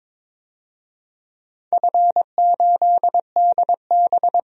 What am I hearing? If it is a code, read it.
Morse code, heard as a keyed tone, about 22 wpm, 700 Hz. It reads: F8DB
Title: DXCC-100 Call Signs - 4 Characters - 22wpm